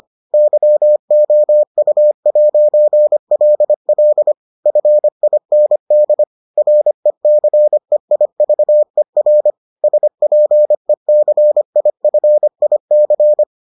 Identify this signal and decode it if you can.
Morse code, heard as a keyed tone, about 25 wpm, 600 Hz. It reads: YOU'LL FIND RECEIVER SPECIFIC